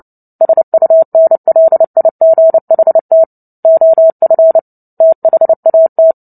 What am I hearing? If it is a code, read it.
Morse code, heard as a keyed tone, about 30 wpm, 650 Hz. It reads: SUNLIGHT OF THAT
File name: Book Two / 03-Chapter 3 - 29 WPM